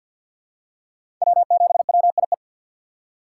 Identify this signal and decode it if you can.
Morse code, heard as a keyed tone, about 50 wpm, 700 Hz. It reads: W7WIE
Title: USA Call Signs - 5 Characters - 6x Character Spacing - 50wpm